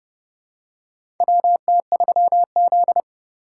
Morse code, heard as a keyed tone, about 32 wpm, 700 Hz. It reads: WT3Z
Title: USA Call Signs - 2x Character Spacing - 30wpm